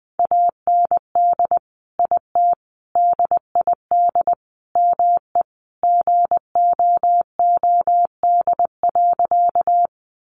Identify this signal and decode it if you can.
Morse code, heard as a keyed tone, about 19 wpm, 700 Hz. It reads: AND IT DID ME GOOD.
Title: Sentences from Top 200 Words 20wpm